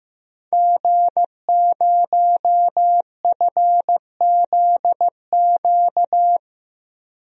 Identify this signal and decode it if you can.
Morse code, heard as a keyed tone, about 16 wpm, 700 Hz. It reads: G0FZQ
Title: DXCC-100 Call Signs - 5 Characters - 15wpm